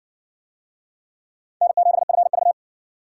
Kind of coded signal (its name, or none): Morse code